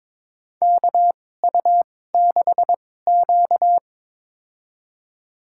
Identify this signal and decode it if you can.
Morse code, heard as a keyed tone, about 22 wpm, spaced as for 16 wpm, 700 Hz. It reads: KU6Q
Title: USA Call Signs - 4 Characters - 2x Character Spacing - 22wpm